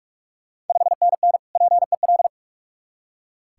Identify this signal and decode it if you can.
Morse code, heard as a keyed tone, about 43 wpm, 690 Hz. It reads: 5NN PEL
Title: Ontario QSO Party Exchange 45wpm